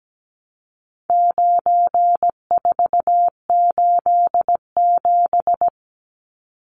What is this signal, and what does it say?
Morse code, heard as a keyed tone, about 17 wpm, 700 Hz. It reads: 9487
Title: Numbers - 4-Digit 17wpm